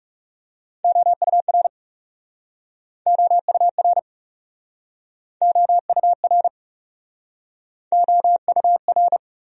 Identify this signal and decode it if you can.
Morse code, heard as a keyed tone, about 40 wpm, 700 Hz. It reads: OUR OUR OUR OUR